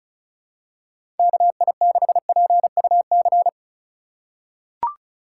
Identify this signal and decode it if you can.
Morse code, heard as a keyed tone, about 36 wpm, 700 Hz. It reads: KI6PUC
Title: USA Call Signs - 6 Characters - 2x Character Spacing - 35wpm